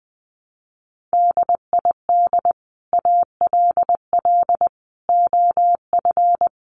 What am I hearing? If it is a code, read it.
Morse code, heard as a keyed tone, about 20 wpm, 700 Hz. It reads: DID ALL OF
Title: A Sound Of Thunder 20wpm